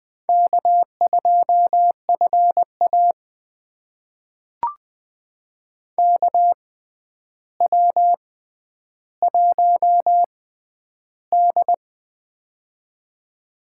Morse code, heard as a keyed tone, about 20 wpm, spaced as for 10 wpm, 700 Hz. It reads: K2FA KW1D
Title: USA Call Signs - 6x Character Spacing - 20wpm